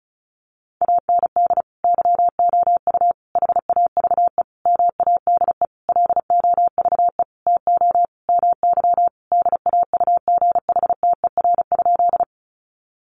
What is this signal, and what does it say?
Morse code, heard as a keyed tone, about 34 wpm, 700 Hz. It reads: AND YOU HAVE MADE LOVE TO MY DAUGHTER?
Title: Sentences from Top 1500 Words 35wpm